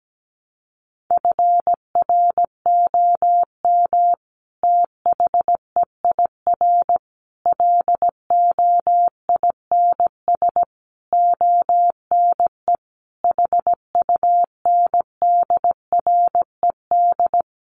Morse code, heard as a keyed tone, about 17 wpm, 700 Hz. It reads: FROM THEIR LOINS ONE HUNDRED
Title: A Sound Of Thunder 17wpm